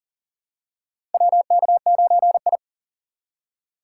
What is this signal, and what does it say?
Morse code, heard as a keyed tone, about 40 wpm, 700 Hz. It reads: WK9I
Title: USA Call Signs - 4 Characters - 6x Character Spacing - 40wpm